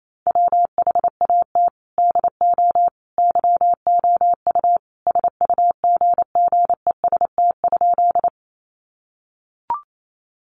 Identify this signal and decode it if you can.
Morse code, heard as a keyed tone, about 28 wpm, 700 Hz. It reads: WHAT DO YOU SUGGEST?
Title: Sentences from Top 1300 Words 28wpm